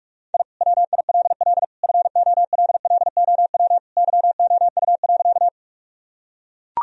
Morse code, heard as a keyed tone, about 44 wpm, 700 Hz. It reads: I WILL FOLLOW YOU.
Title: Sentences from Top 1000 Words - Review Part 1 - 45wpm